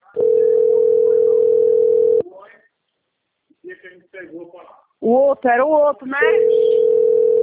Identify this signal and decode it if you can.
Ringback tone (North America), 440 + 480 Hz, with speech.
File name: rg-1600-unknown- - - .61945